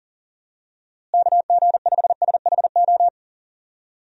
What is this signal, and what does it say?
Morse code, heard as a keyed tone, about 40 wpm, 700 Hz. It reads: KG5SHO
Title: USA Call Signs - 6 Characters - Repeated 2x - 40wpm